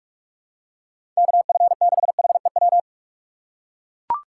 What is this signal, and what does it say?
Morse code, heard as a keyed tone, about 42 wpm, 700 Hz. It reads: KF6HEW
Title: USA Call Signs - 6 Characters - 4x Character Spacing - 45wpm